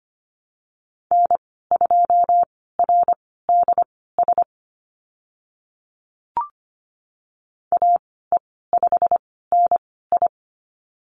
Morse code, heard as a keyed tone, about 25 wpm, spaced as for 16 wpm, 700 Hz. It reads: N2RDS AE5NI
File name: Lesson 15g - Number 5 - Call Signs 25wpm